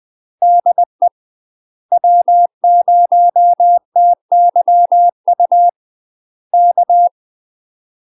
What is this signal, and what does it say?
Morse code, heard as a keyed tone, about 20 wpm, 700 Hz. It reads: DE W0TYU K
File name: UK MASTER QSOs TRAINING / 20WPM / UK MASTER QSOs 0013